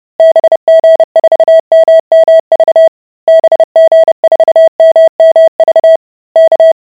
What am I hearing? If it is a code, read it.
Morse code, heard as a keyed tone, about 30 wpm, 650 Hz. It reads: BG4MMV BG4MMV K